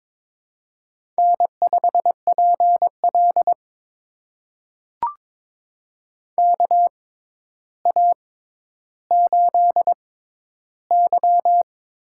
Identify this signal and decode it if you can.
Morse code, heard as a keyed tone, about 23 wpm, spaced as for 11 wpm, 700 Hz. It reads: N5PL KA8Y
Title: USA Call Signs - 6x Character Spacing - 22wpm